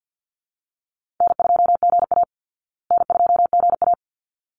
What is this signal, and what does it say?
Morse code, heard as a keyed tone, about 50 wpm, 700 Hz. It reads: N2GA N2GA
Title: Easy Call Signs Repeated 2x - 50wpm